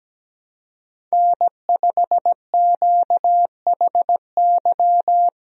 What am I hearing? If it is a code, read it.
Morse code, heard as a keyed tone, about 17 wpm, 700 Hz. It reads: N5QHY
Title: USA Call Signs - 5 Characters - Repeated 3x - 17wpm